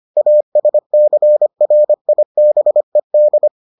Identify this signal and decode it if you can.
Morse code, heard as a keyed tone, about 25 wpm, 600 Hz. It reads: ASCRIBED